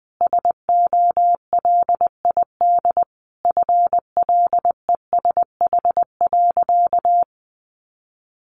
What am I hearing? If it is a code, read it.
Morse code, heard as a keyed tone, about 20 wpm, 700 Hz. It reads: SOLID FLESH.
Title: A Sound Of Thunder 20wpm